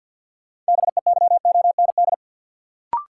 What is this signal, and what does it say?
Morse code, heard as a keyed tone, about 53 wpm, 700 Hz. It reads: BEYOND
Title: Sentences from Top 1100 Words - New Word Drill Alt - 50wpm